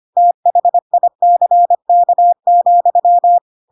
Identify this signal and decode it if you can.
Morse code, heard as a keyed tone, about 25 wpm, 700 Hz. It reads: THICK,
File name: Pinocchio / Pinocchio ch1 25 25 ik1hge